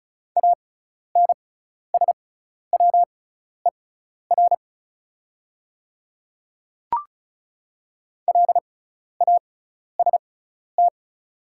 Morse code, heard as a keyed tone, about 36 wpm, spaced as for 12 wpm, 690 Hz. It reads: ANSWER LAST
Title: 6x Character Spacing - Top 300 Words - 35wpm